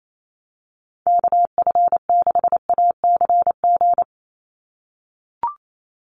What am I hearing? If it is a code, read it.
Morse code, heard as a keyed tone, about 28 wpm, 700 Hz. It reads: KF6ACG
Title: USA Call Signs 28wpm